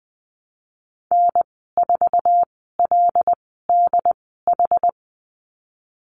Morse code, heard as a keyed tone, about 20 wpm, spaced as for 15 wpm, 700 Hz. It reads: N4LDH
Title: 122 Lesson 20g - Number 3 - Call Signs 20wpm